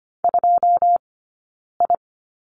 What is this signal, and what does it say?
Morse code, heard as a keyed tone, about 25 wpm, 700 Hz. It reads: 2 I
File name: Lesson 17d - Letter u - Two Characters 25wpm